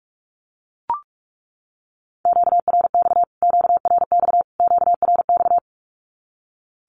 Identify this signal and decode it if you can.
Morse code, heard as a keyed tone, about 44 wpm, 700 Hz. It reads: QRX QRX QRX
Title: Q-Codes Repeated 3x 45wpm